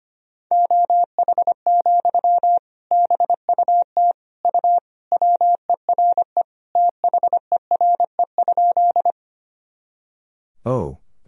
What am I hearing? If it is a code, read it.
Morse code, heard as a keyed tone, about 25 wpm, 700 Hz. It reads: OH, BUT U WERE THERE?
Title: Sentences from Top 500 Words and Abbreviations 25wpm